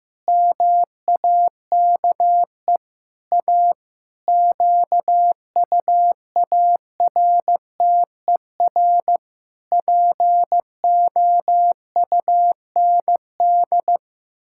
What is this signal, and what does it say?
Morse code, heard as a keyed tone, about 15 wpm, 700 Hz. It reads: MAKE A QUARTER POUND
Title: Pangram Challenge v2 - 15wpm